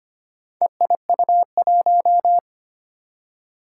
Morse code, heard as a keyed tone, about 25 wpm, 700 Hz. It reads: EIU1
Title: Four Challenging Letter-Number Combinations 25wpm